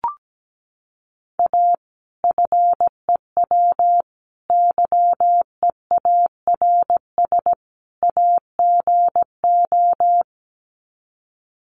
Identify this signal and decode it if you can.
Morse code, heard as a keyed tone, about 17 wpm, 700 Hz. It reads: A FEW YEARS AGO
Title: Sets of 4 Words 17wpm